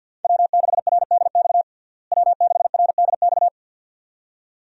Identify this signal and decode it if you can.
Morse code, heard as a keyed tone, about 49 wpm, 700 Hz. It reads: W6RDX W6RDX
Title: USA Call Signs - 5 Characters - Repeated 2x - 50wpm